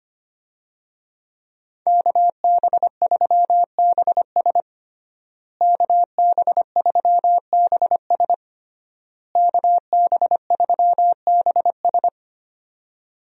Morse code, heard as a keyed tone, about 25 wpm, 700 Hz. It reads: KB3BS KB3BS KB3BS
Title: USA Call Signs - 5 Characters - Repeated 3x - 25wpm